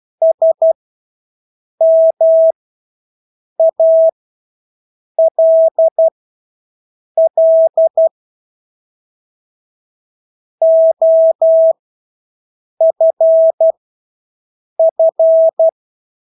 Morse code, heard as a keyed tone, about 12 wpm, spaced as for 6 wpm, 650 Hz. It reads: SMALL OFF
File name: Engelske ord 12-6 wpm